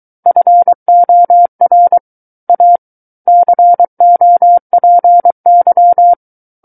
Morse code, heard as a keyed tone, about 23 wpm, 700 Hz. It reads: FOR A COPY
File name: Book One / 14-Chapter 14 - 23 WPM